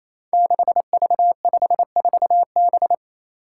Morse code, heard as a keyed tone, about 27 wpm, 700 Hz. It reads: 6V54B
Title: Five Challenging Letter-Number Combinations 28wpm